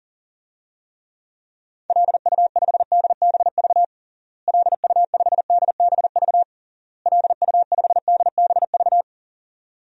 Morse code, heard as a keyed tone, about 40 wpm, 700 Hz. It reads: LU5DBV LU5DBV LU5DBV
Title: DXCC-100 Call Signs - 6 Characters - Repeated 3x - 40wpm